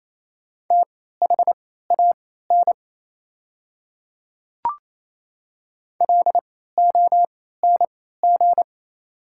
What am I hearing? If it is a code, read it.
Morse code, heard as a keyed tone, about 28 wpm, spaced as for 16 wpm, 700 Hz. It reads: THAN LONG
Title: 3x Character Spacing - Top 300 Words - 28wpm